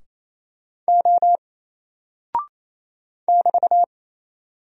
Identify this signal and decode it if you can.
Morse code, heard as a keyed tone, about 27 wpm, 700 Hz. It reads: O <BT>
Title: Single Character (Letter:Number:Punctuation:Prosign) Alt 28wpm